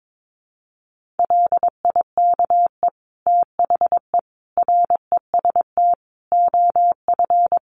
Morse code, heard as a keyed tone, about 22 wpm, 700 Hz. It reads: LIKE THE REST OF
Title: Sentences from Top 1400 Words 22wpm